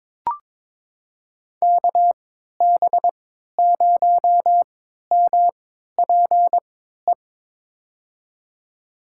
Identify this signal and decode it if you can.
Morse code, heard as a keyed tone, about 22 wpm, spaced as for 13 wpm, 700 Hz. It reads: KB0MPE
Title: USA Call Signs - 3x Character Spacing - 22wpm